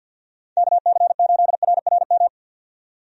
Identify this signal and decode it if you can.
Morse code, heard as a keyed tone, about 51 wpm, 700 Hz. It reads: KC8RRM
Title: USA Call Signs - 6 Characters - 50wpm